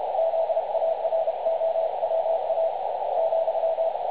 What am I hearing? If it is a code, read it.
Morse code, heard as a keyed tone, about 30 wpm, 650 Hz, with radio noise. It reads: ZAROWA ,
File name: dlazaawansowanych / nagraniazpasma / websdr recording 3565.0kHz